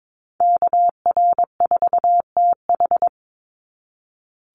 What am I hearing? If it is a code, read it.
Morse code, heard as a keyed tone, about 22 wpm, 700 Hz. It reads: KR4TH
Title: Call Signs - 22wpm